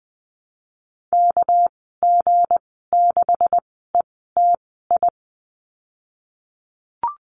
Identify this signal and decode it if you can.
Morse code, heard as a keyed tone, about 20 wpm, spaced as for 15 wpm, 700 Hz. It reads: KG6ETI
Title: 207 Lesson 33g - Letter j - Call Signs 20wpm